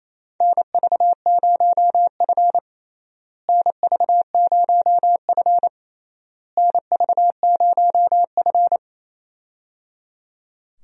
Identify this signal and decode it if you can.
Morse code, heard as a keyed tone, about 27 wpm, 700 Hz. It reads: NV0F NV0F NV0F
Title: USA Call Signs - 4 Characters - Repeated 3x - 28wpm